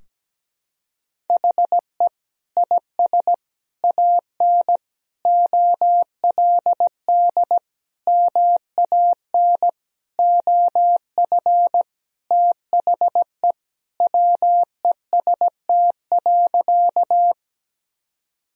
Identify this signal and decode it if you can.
Morse code, heard as a keyed tone, about 17 wpm, 700 Hz. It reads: HE IS AN OLD MAN OF THE WEST.
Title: Sentences from Top 1000 Words - Review Part 2 - 17wpm